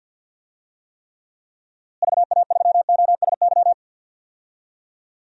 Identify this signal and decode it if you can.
Morse code, heard as a keyed tone, about 53 wpm, 700 Hz. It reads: VA3OSY
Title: DXCC-100 Call Signs - 6 Characters - Repeated 2x - 50wpm